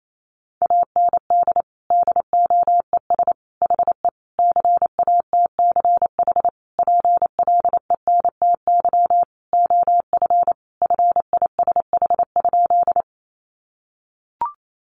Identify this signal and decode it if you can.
Morse code, heard as a keyed tone, about 28 wpm, 700 Hz. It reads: AND DOES HE CATCH PLENTY OF FISH?
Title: Sentences from Top 1500 Words 28wpm